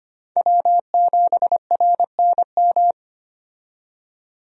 Morse code, heard as a keyed tone, about 25 wpm, 700 Hz. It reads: W7RNM
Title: USA Call Signs - 5 Characters - 25wpm